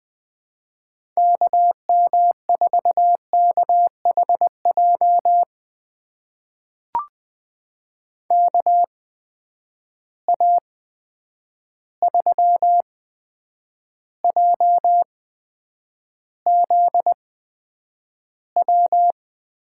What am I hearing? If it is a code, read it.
Morse code, heard as a keyed tone, about 21 wpm, spaced as for 9 wpm, 700 Hz. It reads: KM4KHJ KA3JZW
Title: USA Call Signs - 6 Characters - 8x Character Spacing - 20wpm